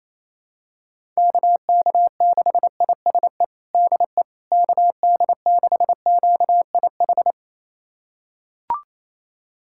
Morse code, heard as a keyed tone, about 27 wpm, 700 Hz. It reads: KK6ISE DE KD6QIH
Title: USA Callsign DE Callsign - 6 Characters - 28wpm